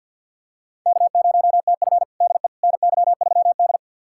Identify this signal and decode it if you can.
Morse code, heard as a keyed tone, about 53 wpm, 700 Hz. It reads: K0TF DE NY3D
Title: USA Callsign DE Callsign - 4 Characters - 50wpm